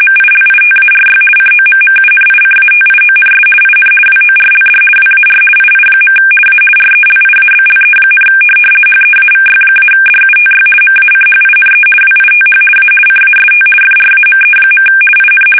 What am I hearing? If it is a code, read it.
RTTY, about 50 baud. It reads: MILITARY COMMUNICATIONS IN WORLD WAR II. A RELIABLE, HEAVY-DUTY MACHINE WITH A CAST FRAME. IN
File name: fsk850-50B / teletype-corp